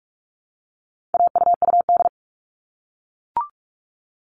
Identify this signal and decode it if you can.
Morse code, heard as a keyed tone, about 44 wpm, 700 Hz. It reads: AUUD
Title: All Cut Numbers - 4-Digit 45wpm